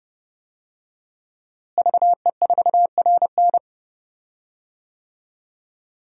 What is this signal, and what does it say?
Morse code, heard as a keyed tone, about 30 wpm, 700 Hz. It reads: VE4RN
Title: DXCC-100 Call Signs - 5 Characters - Repeated 3x - 30wpm